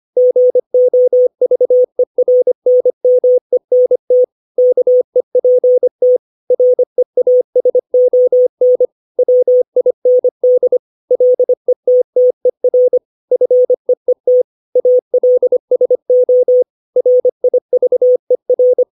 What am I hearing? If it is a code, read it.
Morse code, heard as a keyed tone, about 25 wpm, 500 Hz. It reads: GOVERNMENT KEPT REASON WIND LETTER FEET ALSO RIVER
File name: 632 words / 632 Words 25WPM